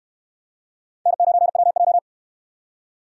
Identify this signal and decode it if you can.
Morse code, heard as a keyed tone, about 69 wpm, 700 Hz. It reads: N9WJ